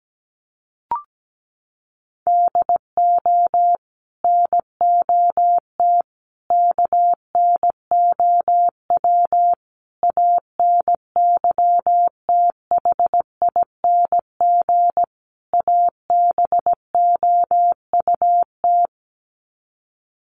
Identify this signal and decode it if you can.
Morse code, heard as a keyed tone, about 17 wpm, 700 Hz. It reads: DO NOT KNOW ANYTHING ABOUT
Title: Sets of 5 Words 17wpm